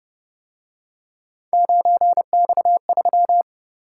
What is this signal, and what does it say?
Morse code, heard as a keyed tone, about 29 wpm, 700 Hz. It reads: 9X3